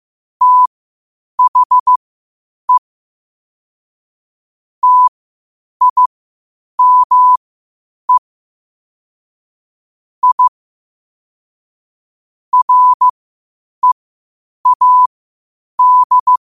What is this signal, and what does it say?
Morse code, heard as a keyed tone, about 15 wpm, spaced as for 8 wpm, 1000 Hz. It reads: THE TIME I READ
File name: qotdfiles / 8wpm